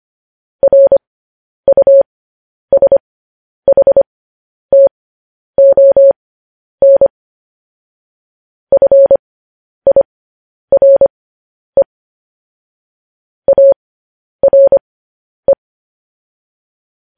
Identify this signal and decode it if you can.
Morse code, heard as a keyed tone, about 25 wpm, spaced as for 10 wpm, 550 Hz. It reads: RUSHTON FIRE ARE
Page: news-headlines-in-morse-code-at-10-wpm